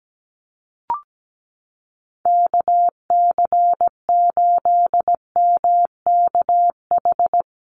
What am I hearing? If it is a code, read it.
Morse code, heard as a keyed tone, about 17 wpm, 700 Hz. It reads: KC8MKH
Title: USA Call Signs Repeated 2x 17wpm